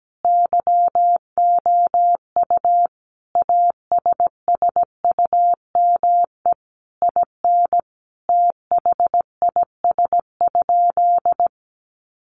Morse code, heard as a keyed tone, about 17 wpm, 700 Hz. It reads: YOU ASSUME IN THIS?
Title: Sentences from Top 1500 Words 17wpm